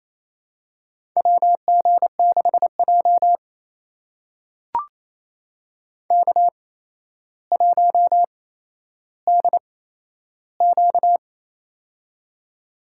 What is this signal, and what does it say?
Morse code, heard as a keyed tone, about 28 wpm, spaced as for 12 wpm, 700 Hz. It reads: WG6J K1DQ
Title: USA Call Signs - 4 Characters - 8x Character Spacing - 28wpm